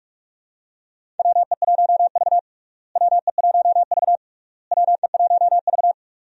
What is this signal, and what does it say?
Morse code, heard as a keyed tone, about 46 wpm, 700 Hz. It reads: WE1V WE1V WE1V
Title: USA Call Signs - 4 Characters - Repeated 3x - 45wpm